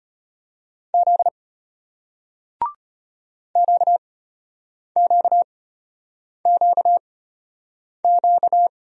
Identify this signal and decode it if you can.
Morse code, heard as a keyed tone, about 34 wpm, spaced as for 8 wpm, 700 Hz. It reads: Z QQQQ